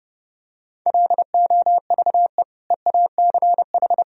Morse code, heard as a keyed tone, about 30 wpm, 700 Hz. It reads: LOVE EACH
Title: Sentences from Top 1000 Words - Review Part 1 - 30wpm